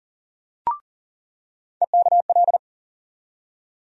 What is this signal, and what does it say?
Morse code, heard as a keyed tone, about 39 wpm, 700 Hz. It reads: EKL
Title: USA Call Sign Postfixes - 3 Letters 40wpm